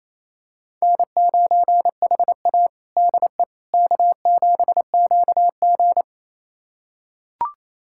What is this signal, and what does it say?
Morse code, heard as a keyed tone, about 28 wpm, 700 Hz. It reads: N9HA DE K7QG
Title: USA Callsign DE Callsign - 4 Characters - 28wpm